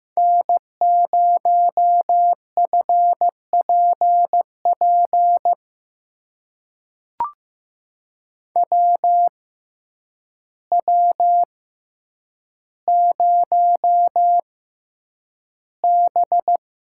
Morse code, heard as a keyed tone, about 15 wpm, spaced as for 8 wpm, 700 Hz. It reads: N0FPP WW0B
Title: USA Call Signs - 6x Character Spacing - 15wpm